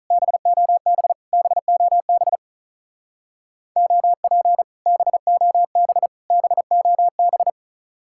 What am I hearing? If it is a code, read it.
Morse code, heard as a keyed tone, about 45 wpm, 700 Hz. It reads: BOB BOB OP BOB BOB